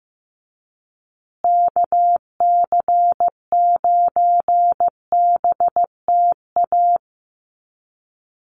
Morse code, heard as a keyed tone, about 15 wpm, 700 Hz. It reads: KC9BTA
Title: USA Call Signs Repeated 2x 15wpm